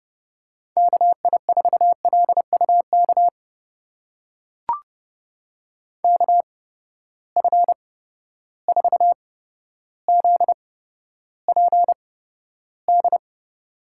Morse code, heard as a keyed tone, about 30 wpm, spaced as for 13 wpm, 700 Hz. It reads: KI4LUK KF4ZPD
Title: USA Call Signs - 6 Characters - 8x Character Spacing - 30wpm